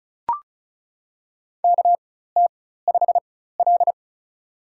Morse code, heard as a keyed tone, about 36 wpm, spaced as for 16 wpm, 700 Hz. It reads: KT5L
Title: USA Call Signs - 4 Characters - 4x Character Spacing - 35wpm